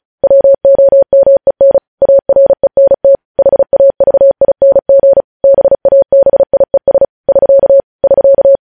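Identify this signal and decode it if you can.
Morse code, heard as a keyed tone, about 36 wpm, 550 Hz. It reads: WOMEN ARENT HAVING BABIES <SK> <SK>
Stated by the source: News headlines sent in Morse code at 35 WPM.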